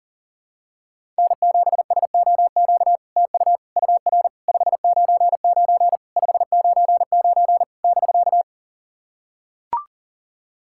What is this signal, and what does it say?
Morse code, heard as a keyed tone, about 43 wpm, 700 Hz. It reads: N7SOQ TU UR 599 599 <BK>
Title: SOTA QSO Practice 40wpm